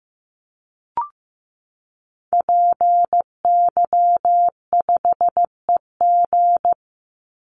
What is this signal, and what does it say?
Morse code, heard as a keyed tone, about 15 wpm, 700 Hz. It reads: PY5EG
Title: Call Signs Repeated 2x - 15wpm